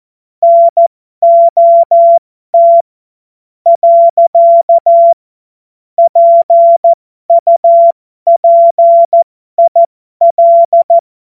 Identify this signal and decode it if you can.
Morse code, heard as a keyed tone, about 14 wpm, spaced as for 11 wpm, 680 Hz. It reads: NOT. PUPIL
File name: Morsekurs / Witze